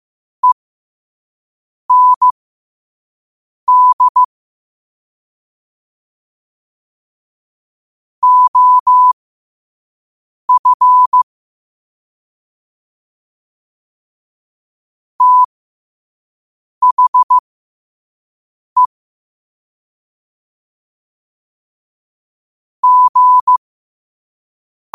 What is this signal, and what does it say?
Morse code, heard as a keyed tone, about 15 wpm, spaced as for 5 wpm, 1000 Hz. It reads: END OF THE G
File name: qotdfiles / 5wpm